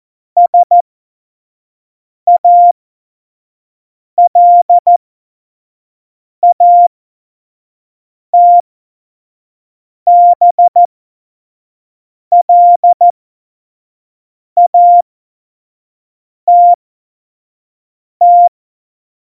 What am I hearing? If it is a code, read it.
Morse code, heard as a keyed tone, about 14 wpm, spaced as for 5 wpm, 700 Hz. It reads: SALATBLATT
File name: Morsekurs / Witze / Witz 29. Oktober 2019, 25 F70